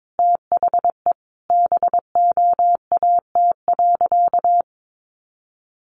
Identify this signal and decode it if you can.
Morse code, heard as a keyed tone, about 22 wpm, 700 Hz. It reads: THE BOAT.
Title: Sentences from Top 1500 Words 22wpm